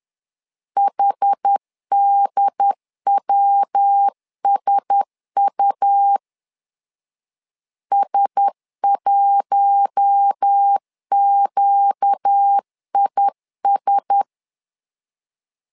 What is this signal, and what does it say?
Morse code, heard as a keyed tone, about 10 wpm, 800 Hz. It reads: HDWSU S1QIS
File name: LEZIONI / 4 Q1S 10wpm 2133